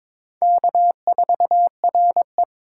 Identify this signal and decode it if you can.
Morse code, heard as a keyed tone, about 21 wpm, 700 Hz. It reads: K4RE
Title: USA Call Signs - 2x Character Spacing - 22wpm